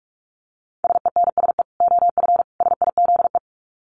Morse code, heard as a keyed tone, about 44 wpm, 700 Hz. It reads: SENSE OF SIZE
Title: Dit Challenge 45wpm